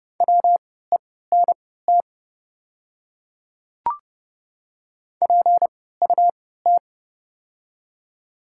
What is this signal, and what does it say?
Morse code, heard as a keyed tone, about 30 wpm, spaced as for 17 wpm, 700 Hz. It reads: WENT PUT
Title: 3x Character Spacing - Top 200 Words - 30wpm